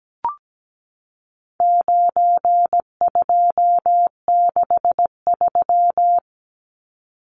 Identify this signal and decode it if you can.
Morse code, heard as a keyed tone, about 17 wpm, 700 Hz. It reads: 9263
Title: Numbers - 4-Digit 17wpm